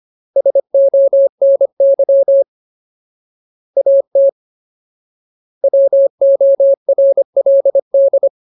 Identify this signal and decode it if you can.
Morse code, heard as a keyed tone, about 25 wpm, 550 Hz. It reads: SONY AT WORLD
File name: Ultimate Word List / 25WPM Ultimate